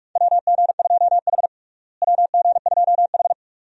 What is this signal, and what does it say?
Morse code, heard as a keyed tone, about 46 wpm, 700 Hz. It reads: WG2H WG2H
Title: USA Call Signs - 4 Characters - Repeated 2x - 45wpm